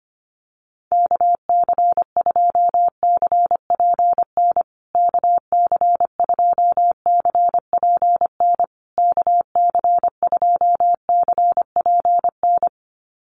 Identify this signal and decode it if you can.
Morse code, heard as a keyed tone, about 25 wpm, 700 Hz. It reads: KC2CPN KC2CPN KC2CPN
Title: USA Call Signs Repeated 3x 25wpm